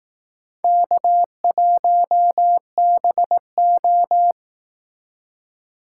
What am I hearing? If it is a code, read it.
Morse code, heard as a keyed tone, about 18 wpm, 700 Hz. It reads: K1BO
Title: Speed Racing - USA Call Signs 15wpm